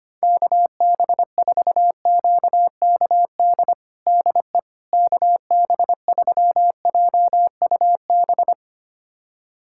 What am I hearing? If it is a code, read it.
Morse code, heard as a keyed tone, about 25 wpm, 700 Hz. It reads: KB4QKD DE KB3JUB
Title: USA Callsign DE Callsign - 6 Characters - 25wpm